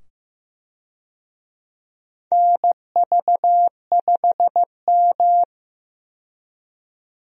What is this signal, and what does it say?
Morse code, heard as a keyed tone, about 16 wpm, 700 Hz. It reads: NV5M
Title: USA Call Signs - 4 Characters - 15wpm